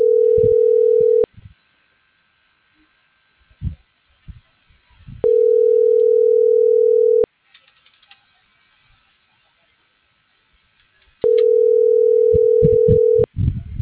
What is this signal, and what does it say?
Ringback tone (North America), 440 + 480 Hz.
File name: CENTRAL BLN / out ericka.canales